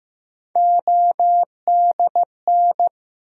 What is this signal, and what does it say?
Morse code, heard as a keyed tone, about 15 wpm, 700 Hz. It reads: ODN
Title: USA Call Sign Postfixes - 3 Letters 15wpm